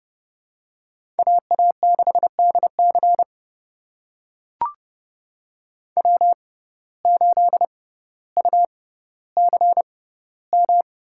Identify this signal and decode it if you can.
Morse code, heard as a keyed tone, about 30 wpm, spaced as for 15 wpm, 700 Hz. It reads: AA6DC W8UCM
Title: USA Call Signs - 5 Characters - 6x Character Spacing - 30wpm